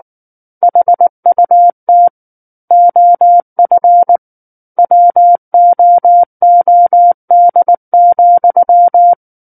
Morse code, heard as a keyed tone, about 19 wpm, 700 Hz. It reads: HUT OF WOOD,
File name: Book One / 10-Chapter 10 - 19 WPM